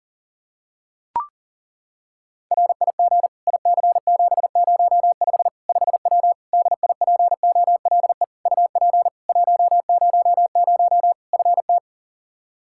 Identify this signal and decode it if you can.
Morse code, heard as a keyed tone, about 40 wpm, 700 Hz. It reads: RIG IC705 5W DIPOLE UP 100 FT
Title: K1USN SST QSO Practice 40wpm